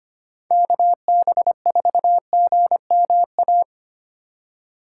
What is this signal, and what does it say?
Morse code, heard as a keyed tone, about 28 wpm, 700 Hz. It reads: KB4GMA
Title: Speed Racing - USA 2x3 Call Signs 25wpm